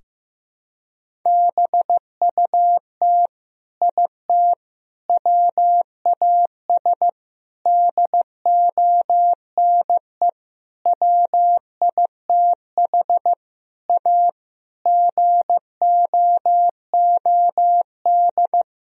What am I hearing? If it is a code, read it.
Morse code, heard as a keyed tone, about 15 wpm, 700 Hz. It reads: BUT IT WAS DONE WITH A GOOD
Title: Sentences from Top 1000 Words - Review Part 4 - 15wpm